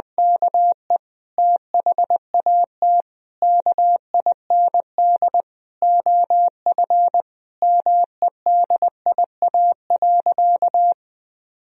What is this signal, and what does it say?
Morse code, heard as a keyed tone, about 21 wpm, 700 Hz. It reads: KE THAT KIND OF MEDIA.
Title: Sentences from Top 1000 Words - Review Part 4 - 20wpm